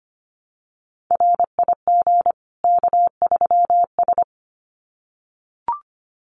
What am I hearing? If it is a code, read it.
Morse code, heard as a keyed tone, about 25 wpm, 700 Hz. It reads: RIG K3S
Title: Common QSO Phrases 25wpm